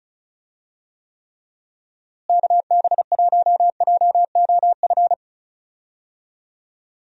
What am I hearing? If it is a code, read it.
Morse code, heard as a keyed tone, about 36 wpm, 700 Hz. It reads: KB1JOF
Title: USA Call Signs - 6 Characters - Repeated 2x - 35wpm